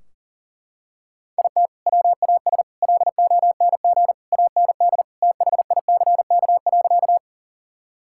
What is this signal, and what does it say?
Morse code, heard as a keyed tone, about 40 wpm, 700 Hz. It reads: IT WAS LONG AND THICK.
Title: Sentences from Top 1600 Words 40wpm